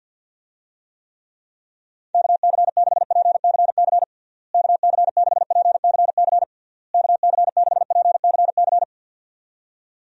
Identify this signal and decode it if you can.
Morse code, heard as a keyed tone, about 52 wpm, 700 Hz. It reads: KC6PCC KC6PCC KC6PCC
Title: USA Call Signs - 6 Characters - Repeated 3x - 50wpm